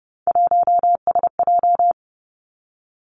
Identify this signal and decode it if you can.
Morse code, heard as a keyed tone, about 30 wpm, 700 Hz. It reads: 1SJ
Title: Three Challenging Letter-Number Combinations 30wpm